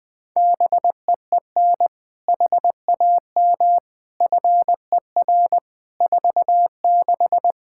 Morse code, heard as a keyed tone, about 20 wpm, 700 Hz. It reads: BEEN HAM FER 46
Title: Practice QSO - Exchange 3 - More Info - 20wpm